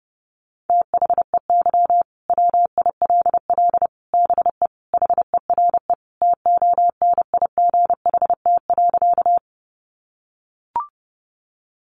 Morse code, heard as a keyed tone, about 30 wpm, 700 Hz. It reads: THEY WILL BE HERE TONIGHT.
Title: Sentences from Top 600 Words 30wpm